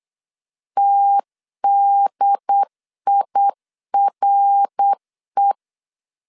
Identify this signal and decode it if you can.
Morse code, heard as a keyed tone, about 8 wpm, 800 Hz. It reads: TDIRE